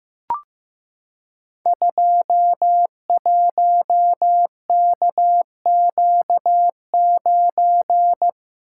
Morse code, heard as a keyed tone, about 15 wpm, 700 Hz. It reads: 21KQ9
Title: Code Groups - 5 Letter:Numbers - 15wpm